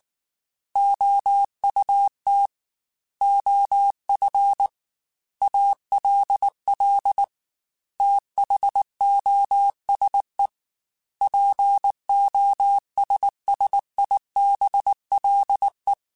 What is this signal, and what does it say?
Morse code, heard as a keyed tone, about 19 wpm, 800 Hz. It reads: OUT OF ALL THOSE POSSIBLE
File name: Technical Subjects / Technical Subjects-mp3s / AND Gate @18